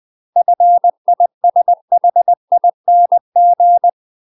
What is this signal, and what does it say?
Morse code, heard as a keyed tone, about 20 wpm, 700 Hz. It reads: FISHING
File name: UK MASTER QSOs TRAINING / 20WPM / UK MASTER QSOs 0013